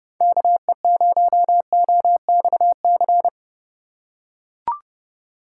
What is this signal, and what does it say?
Morse code, heard as a keyed tone, about 30 wpm, 700 Hz. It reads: KE0OXC
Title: USA Call Signs - 6 Characters - 8x Character Spacing - 30wpm